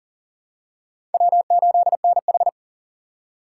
Morse code, heard as a keyed tone, about 40 wpm, 700 Hz. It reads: W8NH
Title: USA Call Signs - 4 Characters - 2x Character Spacing - 40wpm